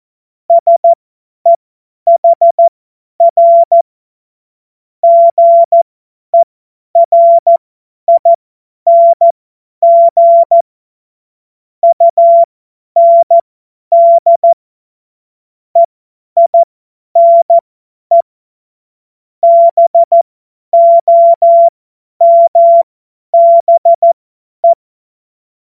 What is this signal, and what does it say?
Morse code, heard as a keyed tone, about 14 wpm, spaced as for 10 wpm, 680 Hz. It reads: SEHR GERING UND EINE BOMBE
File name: Morsekurs / Witze / Witz 05. April 2022 50 F70